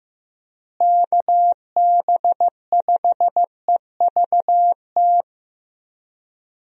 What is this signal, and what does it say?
Morse code, heard as a keyed tone, about 15 wpm, 700 Hz. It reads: KB5EVT
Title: USA Call Signs - 6 Characters - Repeated 3x - 15wpm